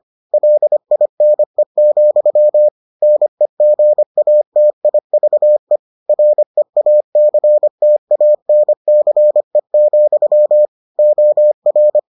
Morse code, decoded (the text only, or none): LINE, NEGATIVE REACTANCE, OR